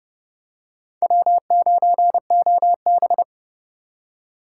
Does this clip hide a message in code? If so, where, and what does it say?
Yes: Morse code from 1.0 to 3.2 s, heard as a keyed tone, about 27 wpm, 700 Hz. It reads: W9OB